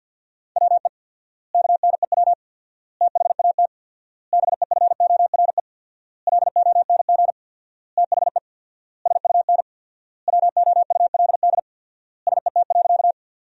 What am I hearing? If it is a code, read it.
Morse code, heard as a keyed tone, about 51 wpm, 700 Hz. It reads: WE KNEW THAT BEFORE LONG THE SUN WOULD SET.
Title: 4x Word-Spacing - Sentences from Top 400 Words - 50wpm